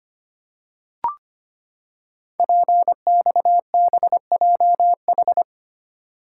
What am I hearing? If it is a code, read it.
Morse code, heard as a keyed tone, about 25 wpm, 700 Hz. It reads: PXBJH
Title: Code Groups - 5 Letters - 25wpm